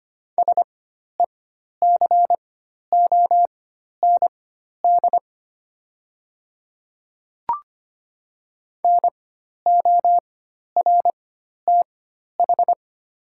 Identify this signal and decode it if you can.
Morse code, heard as a keyed tone, about 25 wpm, spaced as for 12 wpm, 700 Hz. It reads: SECOND NORTH
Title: 4x Character Spacing - Top 500 Words - 25wpm